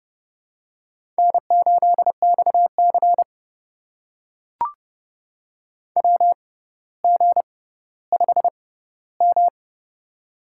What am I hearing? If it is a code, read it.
Morse code, heard as a keyed tone, about 30 wpm, spaced as for 15 wpm, 700 Hz. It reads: N8XC WG5M
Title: USA Call Signs - 6x Character Spacing - 30wpm